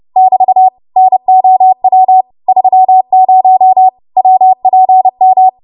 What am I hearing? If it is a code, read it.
Morse code, heard as a keyed tone, about 30 wpm, 750 Hz. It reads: <BT> NOW 30 WPM
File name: Morse / 30WPM